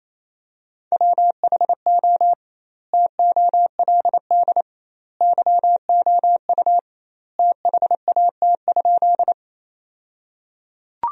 Morse code, heard as a keyed tone, about 28 wpm, 700 Hz. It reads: WHO TOLD YOU THAT?
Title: 2x Word-Spacing - Sentences from Top 400 Words - 28wpm